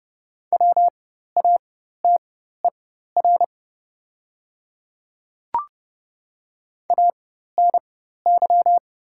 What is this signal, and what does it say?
Morse code, heard as a keyed tone, about 30 wpm, spaced as for 14 wpm, 700 Hz. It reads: WATER ANY
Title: 4x Character Spacing - Top 200 Words - 30wpm